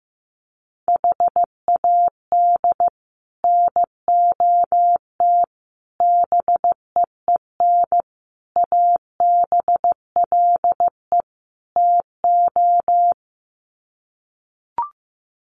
Morse code, heard as a keyed tone, about 15 wpm, 700 Hz. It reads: HAD NOT BEEN ABLE TO
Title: Sets of 5 Words Encore 15wpm